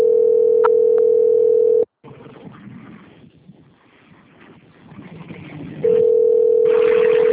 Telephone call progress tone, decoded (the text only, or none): ringback tone